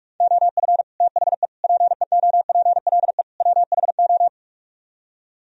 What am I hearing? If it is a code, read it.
Morse code, heard as a keyed tone, about 49 wpm, 700 Hz. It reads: OF THE PEOPLE WHO